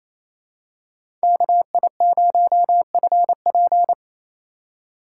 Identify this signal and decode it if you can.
Morse code, heard as a keyed tone, about 28 wpm, 700 Hz. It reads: KI0FP
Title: USA Call Signs - 5 Characters - 8x Character Spacing - 28wpm